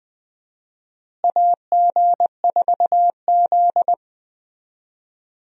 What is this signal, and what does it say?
Morse code, heard as a keyed tone, about 20 wpm, 700 Hz. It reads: AG4Z
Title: USA Call Signs - 4 Characters - Repeated 2x - 20wpm